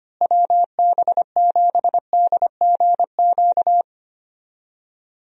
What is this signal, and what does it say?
Morse code, heard as a keyed tone, about 25 wpm, 700 Hz. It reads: WB7DGQ
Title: USA Call Signs - 6 Characters - Repeated 2x - 25wpm